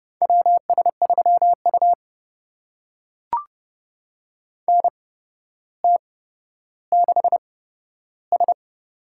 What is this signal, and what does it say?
Morse code, heard as a keyed tone, about 30 wpm, spaced as for 13 wpm, 700 Hz. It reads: WS3U NT6S
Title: USA Call Signs - 4 Characters - 8x Character Spacing - 30wpm